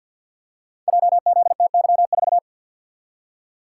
Morse code, heard as a keyed tone, about 50 wpm, 700 Hz. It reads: JZTY4
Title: Code Groups - 5 Letter:Numbers - 50wpm